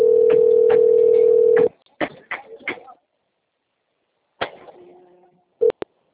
Ringback tone (North America), 440 + 480 Hz.